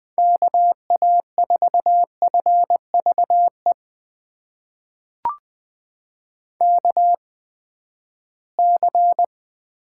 Morse code, heard as a keyed tone, about 20 wpm, spaced as for 14 wpm, 700 Hz. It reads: KA4FVE KC
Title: USA Call Signs - 6 Characters - 8x Character Spacing - 20wpm